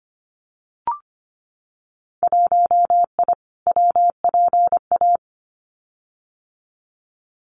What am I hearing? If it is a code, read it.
Morse code, heard as a keyed tone, about 25 wpm, 700 Hz. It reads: 1I WPA
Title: Winter Field Day Exchanges Encore 25wpm